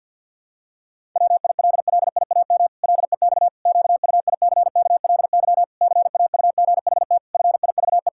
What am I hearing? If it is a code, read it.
Morse code, heard as a keyed tone, about 50 wpm, 700 Hz. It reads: WILLIAM LEX QUICKLY CAUGHT FIVE
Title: Pangram Challenge v2 - 50wpm